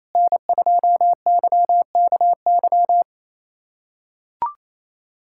Morse code, heard as a keyed tone, about 28 wpm, 700 Hz. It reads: N2YKY
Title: USA Callsign DE Callsign - 5 Characters - 28wpm